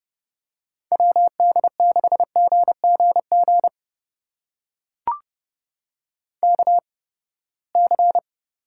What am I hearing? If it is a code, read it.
Morse code, heard as a keyed tone, about 30 wpm, spaced as for 21 wpm, 700 Hz. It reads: WD6GGG KC
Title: USA Call Signs - 6 Characters - 8x Character Spacing - 30wpm